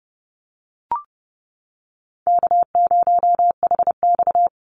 Morse code, heard as a keyed tone, about 30 wpm, 700 Hz. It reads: K0HX
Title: Easy Call Signs - 30wpm